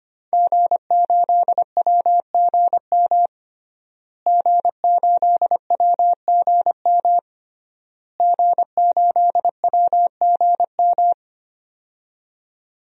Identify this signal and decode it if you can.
Morse code, heard as a keyed tone, about 25 wpm, 700 Hz. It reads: G8WGM G8WGM G8WGM
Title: DXCC-100 Call Signs - 5 Characters - Repeated 3x - 25wpm